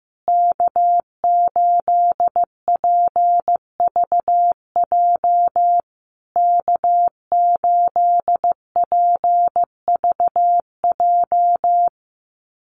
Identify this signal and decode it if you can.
Morse code, heard as a keyed tone, about 15 wpm, 700 Hz. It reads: K8PVJ K8PVJ
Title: USA Call Signs Repeated 3x 15wpm